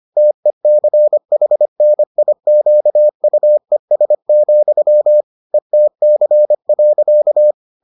Morse code, heard as a keyed tone, about 25 wpm, 600 Hz. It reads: TECHNIQUES, ETC.